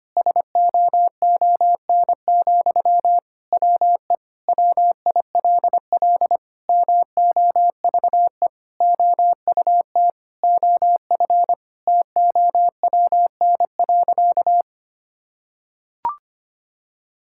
Morse code, heard as a keyed tone, about 25 wpm, 700 Hz. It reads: SOON, WE WILL MOVE OUT OF TOWN.
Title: Sentences from Top 1000 Words - Review Part 6 - 25wpm